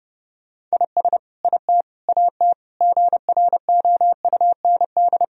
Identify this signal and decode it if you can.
Morse code, heard as a keyed tone, about 30 wpm, 700 Hz. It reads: IS IT AT GROUND
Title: Sentences from Top 1000 Words - Review Part 4 - 30wpm